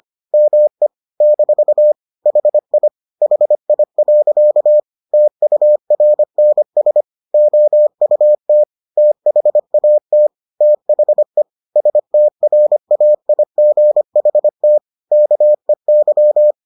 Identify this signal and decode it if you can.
Morse code, heard as a keyed tone, about 25 wpm, 600 Hz. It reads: ME - HI HI. TURNS OUT THAT THE STRAIGHT KEY